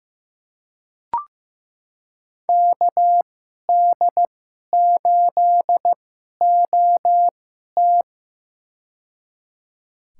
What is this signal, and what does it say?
Morse code, heard as a keyed tone, about 15 wpm, spaced as for 11 wpm, 700 Hz. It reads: KD8OT
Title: USA Call Signs - 5 Characters - 2x Character Spacing - 15wpm